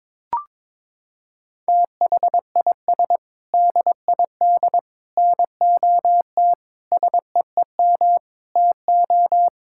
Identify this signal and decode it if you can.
Morse code, heard as a keyed tone, about 22 wpm, 700 Hz. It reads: THIS DID NOT SEEM TO
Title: Sentences from Top 1000 Words - Review Part 6 - 22wpm